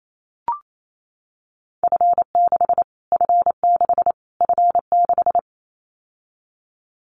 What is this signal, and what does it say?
Morse code, heard as a keyed tone, about 28 wpm, 700 Hz. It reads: F6 F6 F6
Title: Callsign Prefixes for Top 25 DXCC - Repeated 3x - 28wpm